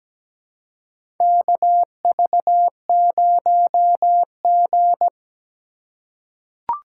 Morse code, heard as a keyed tone, about 17 wpm, 700 Hz. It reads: KV0G
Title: USA Call Signs - 4 Characters - 2x Character Spacing - 17wpm